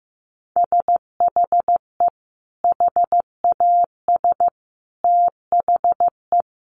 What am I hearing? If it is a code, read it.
Morse code, heard as a keyed tone, about 15 wpm, 700 Hz. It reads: SHE HAS THE
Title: Sentences from Top 1200 Words 15wpm